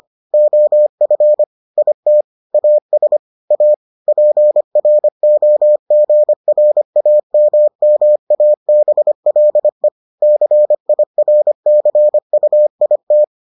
Morse code, heard as a keyed tone, about 25 wpm, 600 Hz. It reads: OF IT AS A PROGRAMMABLE CIRCUIT